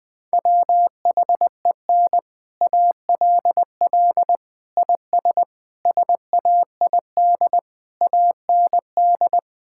Morse code, heard as a keyed tone, about 20 wpm, 700 Hz. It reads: WHEN ALL IS SAID AND
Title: Sentences with 6-Word Phrases - 20wpm